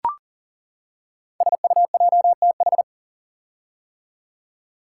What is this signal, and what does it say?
Morse code, heard as a keyed tone, about 39 wpm, 700 Hz. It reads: SUJTH
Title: Code Groups - 5 Letter:Numbers - 40wpm